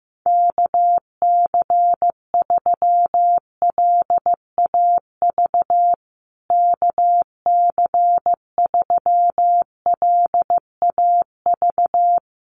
Morse code, heard as a keyed tone, about 15 wpm, 700 Hz. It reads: KC3LAV KC3LAV
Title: USA Call Signs Repeated 3x 15wpm